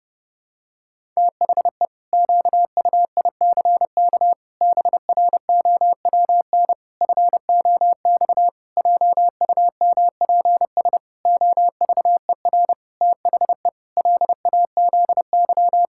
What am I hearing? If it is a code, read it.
Morse code, heard as a keyed tone, about 32 wpm, 700 Hz. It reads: THE QUICK BROWN FOX JUMPS OVER THE LAZY
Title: Pangram Challenge v2 - 30wpm